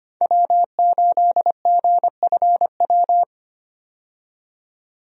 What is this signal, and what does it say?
Morse code, heard as a keyed tone, about 25 wpm, 700 Hz. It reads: W8GFW
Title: USA Call Signs - 5 Characters - 25wpm